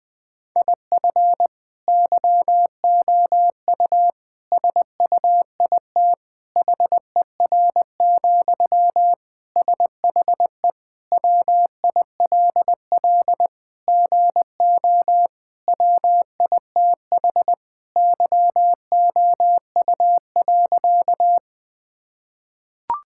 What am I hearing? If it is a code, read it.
Morse code, heard as a keyed tone, about 20 wpm, 700 Hz. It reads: IF YOU SUIT HER, SHE WILL GO WITH YOU.
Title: Sentences from Top 1600 Words 20wpm